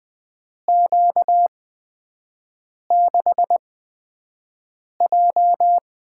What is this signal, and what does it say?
Morse code, heard as a keyed tone, about 20 wpm, spaced as for 5 wpm, 700 Hz. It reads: Q6J
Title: USA Call Signs - 4 Characters - 8x Character Spacing - 20wpm